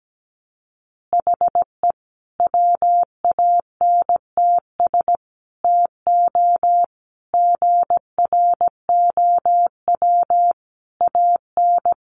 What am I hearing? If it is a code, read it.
Morse code, heard as a keyed tone, about 17 wpm, 700 Hz. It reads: HE WANTS TO GROW AN
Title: Easy Phrases 17wpm